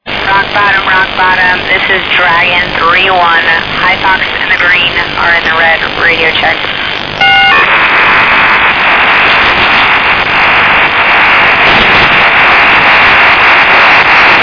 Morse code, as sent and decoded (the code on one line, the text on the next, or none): none